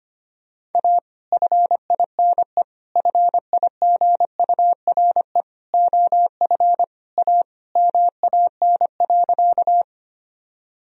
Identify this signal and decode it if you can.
Morse code, heard as a keyed tone, about 25 wpm, 700 Hz. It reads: A FINE FIGURE OF A MAN.
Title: Sentences from Top 1000 Words - Review Part 2 - 25wpm